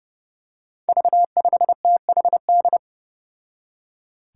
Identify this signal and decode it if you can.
Morse code, heard as a keyed tone, about 30 wpm, 700 Hz. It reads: V5THD
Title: Five Challenging Letter-Number Combinations 30wpm